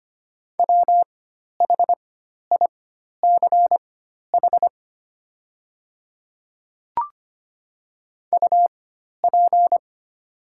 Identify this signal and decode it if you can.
Morse code, heard as a keyed tone, about 25 wpm, spaced as for 12 wpm, 700 Hz. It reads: WHICH UP
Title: 4x Character Spacing - Top 200 Words - 25wpm